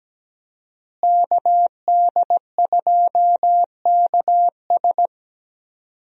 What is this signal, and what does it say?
Morse code, heard as a keyed tone, about 17 wpm, 700 Hz. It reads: KD2KS
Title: USA Call Signs - 5 Characters - Repeated 3x - 17wpm